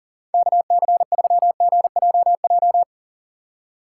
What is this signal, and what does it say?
Morse code, heard as a keyed tone, about 40 wpm, 700 Hz. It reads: KC3GJJ
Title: USA Call Signs - 6 Characters - 2x Character Spacing - 40wpm